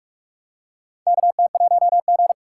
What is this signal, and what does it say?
Morse code, heard as a keyed tone, about 46 wpm, 700 Hz. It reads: KT1G
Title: USA Call Signs - 3x Character Spacing - 45wpm